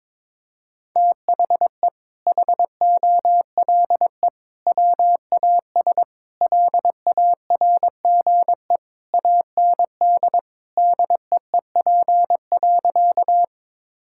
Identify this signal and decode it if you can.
Morse code, heard as a keyed tone, about 22 wpm, 700 Hz. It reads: THE HOLE WAS LARGE AND DEEP.
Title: Sentences from Top 1600 Words 22wpm